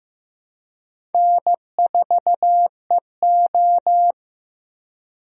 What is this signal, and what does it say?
Morse code, heard as a keyed tone, about 15 wpm, 700 Hz. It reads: N4EO
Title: USA Call Signs - 4 Characters - Repeated 2x - 15wpm